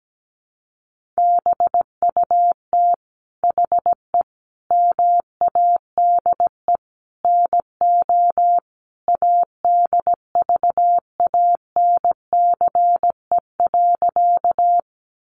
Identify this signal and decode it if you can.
Morse code, heard as a keyed tone, about 17 wpm, 700 Hz. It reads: BUT HE MADE NO ADVANCE.
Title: Sentences from Top 1400 Words 17wpm